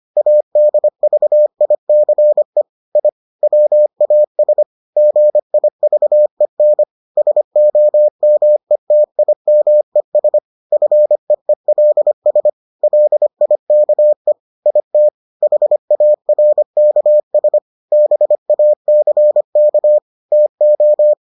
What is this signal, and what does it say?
Morse code, heard as a keyed tone, about 25 wpm, 600 Hz. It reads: ADVICE I WAS GIVEN SOMETIMES FEELS LIKE IT HARKS BACK TO